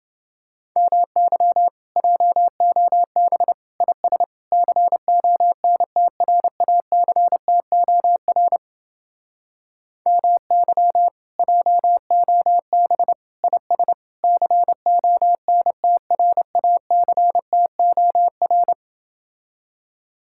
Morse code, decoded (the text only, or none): MY JOB IS CONTRACTOR MY JOB IS CONTRACTOR